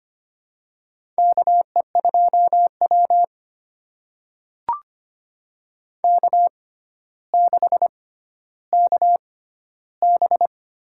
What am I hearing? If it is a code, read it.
Morse code, heard as a keyed tone, about 25 wpm, spaced as for 13 wpm, 700 Hz. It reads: KE2W K6KB
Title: USA Call Signs - 4 Characters - 6x Character Spacing - 25wpm